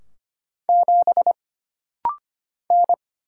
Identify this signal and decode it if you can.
Morse code, heard as a keyed tone, about 25 wpm, 700 Hz. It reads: 7 N
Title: Single Letter-Number Alt - Rapid-Fire 25wpm